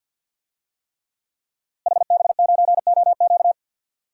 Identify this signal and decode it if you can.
Morse code, heard as a keyed tone, about 49 wpm, 700 Hz. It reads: HB9OQ
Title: DXCC-100 Call Signs - 5 Characters - Repeated 3x - 50wpm